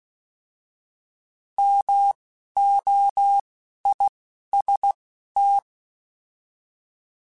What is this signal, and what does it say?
Morse code, heard as a keyed tone, about 16 wpm, spaced as for 11 wpm, 800 Hz. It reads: MOIST
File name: Alphabetical / Alphabetical mp3s / Ms @10